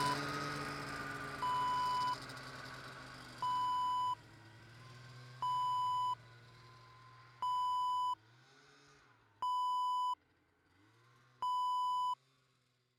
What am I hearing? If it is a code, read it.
There is no code.